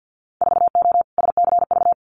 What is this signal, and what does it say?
Morse code, heard as a keyed tone, about 50 wpm, 700 Hz. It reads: 4O SCV
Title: Winter Field Day Exchanges 50wpm